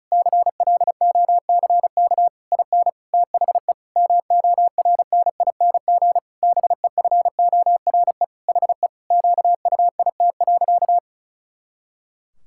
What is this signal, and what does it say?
Morse code, heard as a keyed tone, about 36 wpm, 700 Hz. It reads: CLOCK IN THE MORNING BEFORE HE QUIT.
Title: Pangram Challenge v2 - 35wpm